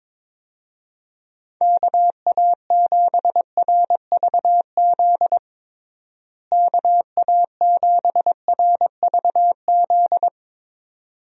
Morse code, heard as a keyed tone, about 22 wpm, 700 Hz. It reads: KA7RVZ KA7RVZ
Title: USA Call Signs - 6 Characters - Repeated 2x - 22wpm